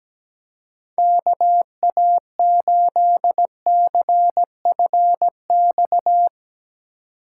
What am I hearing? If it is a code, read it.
Morse code, heard as a keyed tone, about 19 wpm, 700 Hz. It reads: KA8CFX
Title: Speed Racing - USA Call Signs 17wpm